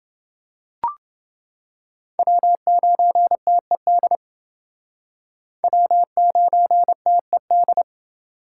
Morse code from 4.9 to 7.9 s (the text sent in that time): W9TED